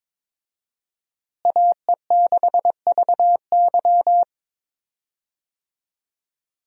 Morse code, heard as a keyed tone, about 22 wpm, 700 Hz. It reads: AE6VY
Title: USA Call Signs - 5 Characters - Repeated 3x - 22wpm